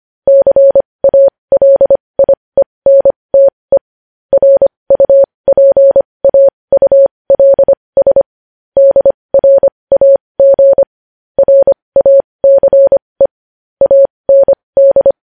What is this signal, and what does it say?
Morse code, heard as a keyed tone, about 25 wpm, spaced as for 20 wpm, 550 Hz. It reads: CALIENTE RUPAULS DRAG RACE AND
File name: podcast / News headlines-1x-25x20